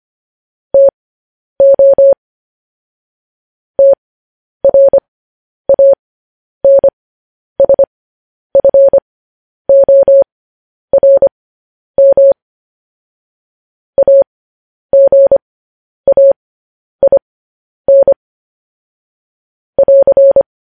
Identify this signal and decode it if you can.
Morse code, heard as a keyed tone, about 25 wpm, spaced as for 10 wpm, 550 Hz. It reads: TO TRANSFORM AGAIN <AR>
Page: news headlines in morse code at 10 wpm